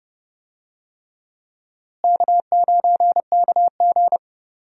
Morse code, heard as a keyed tone, about 30 wpm, 700 Hz. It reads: K9KG
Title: USA Call Signs - 4 Characters - Repeated 3x - 30wpm